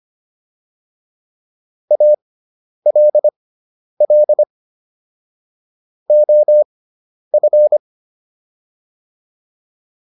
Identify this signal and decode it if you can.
Morse code, heard as a keyed tone, about 25 wpm, spaced as for 10 wpm, 600 Hz. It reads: ALL OF